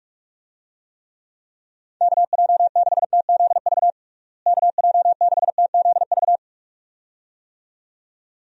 Morse code, heard as a keyed tone, about 41 wpm, 700 Hz. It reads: KJ6TZV KJ6TZV
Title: USA Call Signs - 6 Characters - Repeated 2x - 45wpm